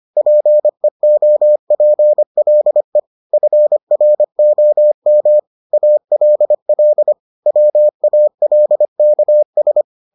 Morse code, heard as a keyed tone, about 25 wpm, 600 Hz. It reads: PEOPLE FROM ALL WALKS